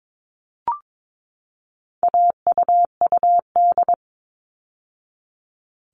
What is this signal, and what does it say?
Morse code, heard as a keyed tone, about 22 wpm, 700 Hz. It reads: AUUD
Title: All Cut Numbers - 4-Digit 22wpm